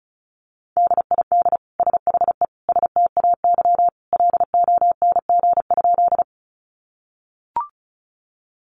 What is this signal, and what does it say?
Morse code, heard as a keyed tone, about 34 wpm, 700 Hz. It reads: DID SHE STAY LONG?
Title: Sentences from Top 500 Words 35wpm